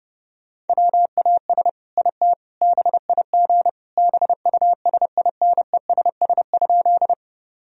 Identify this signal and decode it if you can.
Morse code, heard as a keyed tone, about 32 wpm, 700 Hz. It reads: WAS IT BIG BUSINESS?
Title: Sentences from Top 1000 Words - Review Part 1 - 30wpm